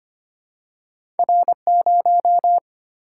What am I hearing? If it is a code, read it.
Morse code, heard as a keyed tone, about 25 wpm, 700 Hz. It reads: R0
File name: Two Letter-Number Combinations - 25wpm